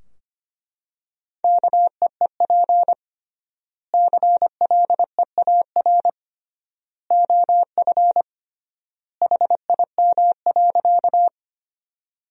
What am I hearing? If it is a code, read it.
Morse code, heard as a keyed tone, about 25 wpm, 700 Hz. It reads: KEEP CLEAR OF HIM.
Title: 3x Word-Spacing - Sentences from Top 500 Words - 25wpm